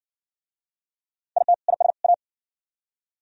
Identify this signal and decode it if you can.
Morse code, heard as a keyed tone, about 55 wpm, 700 Hz. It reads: IT IS A